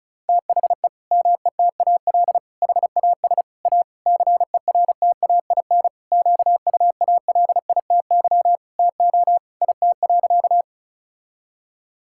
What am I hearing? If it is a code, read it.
Morse code, heard as a keyed tone, about 36 wpm, 700 Hz. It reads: THE METAL HAS A CERTAIN QUALITY TO IT.
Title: Sentences from Top 1000 Words - Review Part 6 - 35wpm